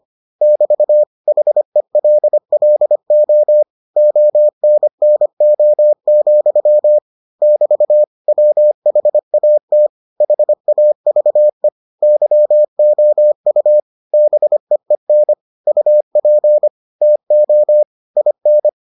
Morse code, heard as a keyed tone, about 25 wpm, 600 Hz. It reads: <BT> HELLO ONNO, <BT> WHAT HAVE YOU BEEN UP TO IN